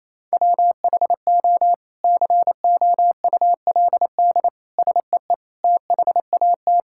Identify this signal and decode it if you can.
Morse code, heard as a keyed tone, about 28 wpm, 700 Hz. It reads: WHO COULD SEE THAT
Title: Sentences from Top 1600 Words 28wpm